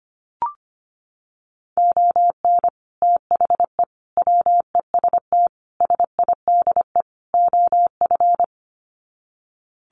Morse code, heard as a keyed tone, about 25 wpm, 700 Hz. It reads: ON THE WEST SIDE OF
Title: Sets of 5 Words Encore2 25wpm